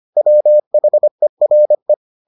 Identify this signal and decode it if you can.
Morse code, heard as a keyed tone, about 25 wpm, 600 Hz. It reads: WHERE